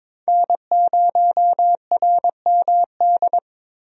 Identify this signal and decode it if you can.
Morse code, heard as a keyed tone, about 22 wpm, 700 Hz. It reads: N0RMD
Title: USA Call Signs - 5 Characters - Repeated 3x - 22wpm